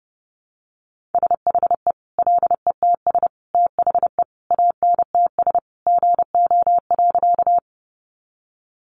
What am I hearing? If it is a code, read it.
Morse code, heard as a keyed tone, about 30 wpm, 700 Hz. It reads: SHE LETS THE ANTS GO.
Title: Easy Phrases 30wpm